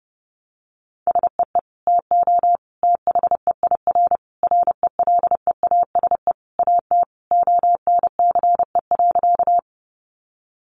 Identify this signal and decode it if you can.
Morse code, heard as a keyed tone, about 30 wpm, 700 Hz. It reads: SEE TO THEIR RELEASE AT ONCE.
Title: Sentences from Top 1100 Words 30wpm